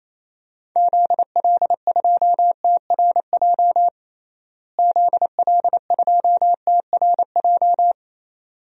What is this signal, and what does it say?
Morse code, heard as a keyed tone, about 28 wpm, 700 Hz. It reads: ZL2TRJ ZL2TRJ
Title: DXCC-100 Call Signs - 6 Characters - Repeated 2x - 28wpm